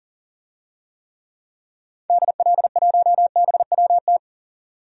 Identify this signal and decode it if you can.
Morse code, heard as a keyed tone, about 40 wpm, 700 Hz. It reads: DL1BWT
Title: DXCC-100 Call Signs - 6 Characters - 40wpm